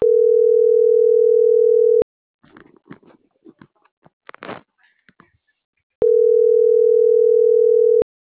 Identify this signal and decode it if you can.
Ringback tone (North America), 440 + 480 Hz.